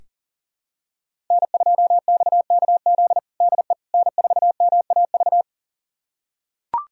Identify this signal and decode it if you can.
Morse code, heard as a keyed tone, about 40 wpm, 700 Hz. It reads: N2XKZ DE N4MAV
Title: USA Callsign DE Callsign - 5 Characters - 40wpm